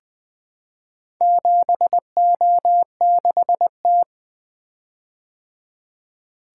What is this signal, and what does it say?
Morse code, heard as a keyed tone, about 21 wpm, 700 Hz. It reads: 7O6T
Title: Four Letter-Number Combinations - 20wpm